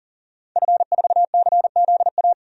Morse code, heard as a keyed tone, about 38 wpm, 700 Hz. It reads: F4CZA
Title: DXCC-100 Call Signs - 5 Characters - 40wpm